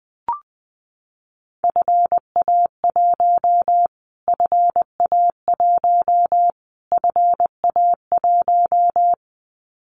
Morse code, heard as a keyed tone, about 20 wpm, 700 Hz. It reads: FA1 FA1 FA1
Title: Callsign Prefixes for Top 25 DXCC - Repeated 3x - 20wpm